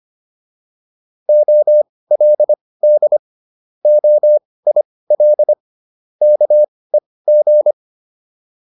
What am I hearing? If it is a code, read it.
Morse code, heard as a keyed tone, about 25 wpm, spaced as for 18 wpm, 600 Hz. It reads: OLD OIL KEG